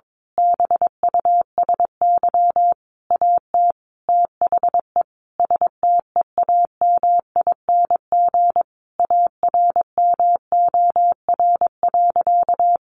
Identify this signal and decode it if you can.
Morse code, heard as a keyed tone, about 22 wpm, 700 Hz. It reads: BUSY AT THE STEAMING ARMOR.
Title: A Sound Of Thunder 22wpm